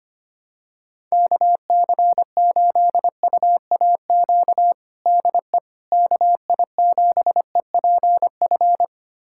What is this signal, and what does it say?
Morse code, heard as a keyed tone, about 25 wpm, 700 Hz. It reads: KC8UAQ DE KI7EPF
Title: USA Callsign DE Callsign - 6 Characters - 25wpm